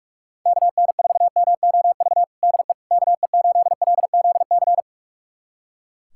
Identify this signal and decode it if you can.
Morse code, heard as a keyed tone, about 42 wpm, 700 Hz. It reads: KN4MOV DE KE8LZC
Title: USA Callsign DE Callsign - 6 Characters - 45wpm